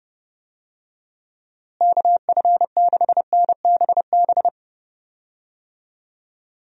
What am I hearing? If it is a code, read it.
Morse code, heard as a keyed tone, about 30 wpm, 700 Hz. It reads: KF6NBB
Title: USA Call Signs - 6 Characters - Repeated 3x - 30wpm